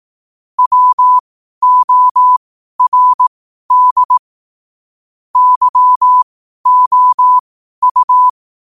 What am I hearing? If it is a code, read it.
Morse code, heard as a keyed tone, about 18 wpm, spaced as for 12 wpm, 1000 Hz. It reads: WORD YOU
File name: qotdfiles / 12wpm